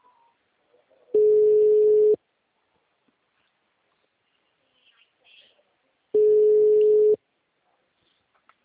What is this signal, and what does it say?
Ringback tone (Europe, 425 Hz), with speech.
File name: out- -2011- - - .44761